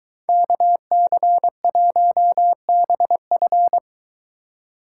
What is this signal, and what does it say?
Morse code, heard as a keyed tone, about 23 wpm, 700 Hz. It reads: KC1BF